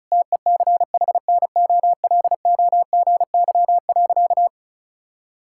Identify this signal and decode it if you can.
Morse code, heard as a keyed tone, about 36 wpm, 700 Hz. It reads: TECHNOLOGY.
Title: Sentences from Top 1000 Words - Review Part 5 - 35wpm